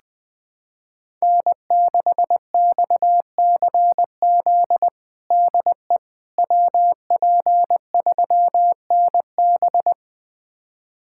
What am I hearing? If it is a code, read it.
Morse code, heard as a keyed tone, about 20 wpm, 700 Hz. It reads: N6XCZ DE WP3NB
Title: USA Callsign DE Callsign - 5 Characters - 20wpm